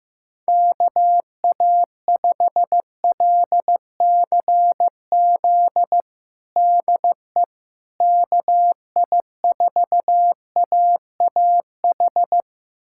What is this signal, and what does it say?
Morse code, heard as a keyed tone, about 15 wpm, 700 Hz. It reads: KA5LCZ DE KI4AAH
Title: USA Callsign DE Callsign - 6 Characters - 15wpm